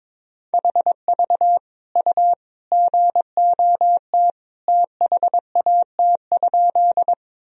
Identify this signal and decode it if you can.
Morse code, heard as a keyed tone, about 22 wpm, 700 Hz. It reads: HV U GOT THAT?
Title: Sentences from Top 300 Words and Abbreviations 22wpm